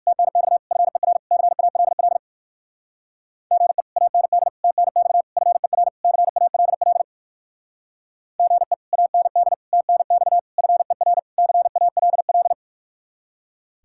Morse code, heard as a keyed tone, about 63 wpm, 700 Hz. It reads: TNX FER CALL GE AND TNX FER CALL GE AND TNX FER CALL